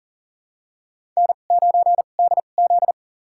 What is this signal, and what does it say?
Morse code, heard as a keyed tone, about 40 wpm, spaced as for 29 wpm, 700 Hz. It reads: N9DZ
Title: USA Call Signs - 4 Characters - 2x Character Spacing - 40wpm